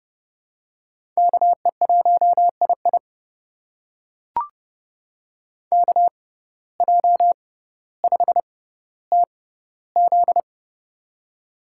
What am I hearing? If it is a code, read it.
Morse code, heard as a keyed tone, about 30 wpm, spaced as for 15 wpm, 700 Hz. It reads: KE1II KJ5TZ
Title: USA Call Signs - 5 Characters - 6x Character Spacing - 30wpm